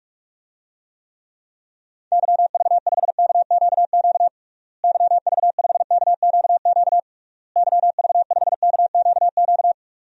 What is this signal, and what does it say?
Morse code, heard as a keyed tone, about 47 wpm, 700 Hz. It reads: YV5KQQ YV5KQQ YV5KQQ
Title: DXCC-100 Call Signs - 6 Characters - Repeated 3x - 45wpm